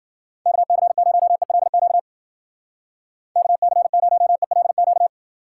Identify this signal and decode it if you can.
Morse code, heard as a keyed tone, about 53 wpm, 700 Hz. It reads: KC0ELQ KC0ELQ